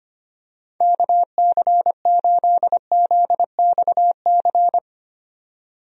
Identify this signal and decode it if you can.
Morse code, heard as a keyed tone, about 25 wpm, 700 Hz. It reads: KC8ZXC
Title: USA Call Signs - 6 Characters - 25wpm